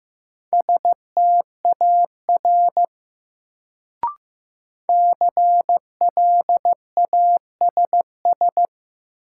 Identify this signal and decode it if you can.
Morse code, heard as a keyed tone, about 15 wpm, 700 Hz. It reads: STAR CLASS
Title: Sentences from Top 1000 Words - Review Part 2 - New Word Drill - 15wpm